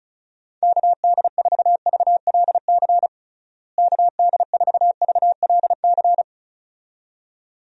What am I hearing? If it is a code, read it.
Morse code, heard as a keyed tone, about 34 wpm, 700 Hz. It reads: KD4VLC KD4VLC
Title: USA Call Signs - 6 Characters - Repeated 2x - 35wpm